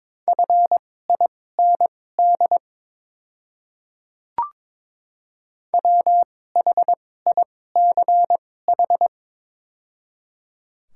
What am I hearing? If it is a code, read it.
Morse code, heard as a keyed tone, about 22 wpm, spaced as for 16 wpm, 700 Hz. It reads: FIND WHICH
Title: 2x Character Spacing - Top 100 Words - 22wpm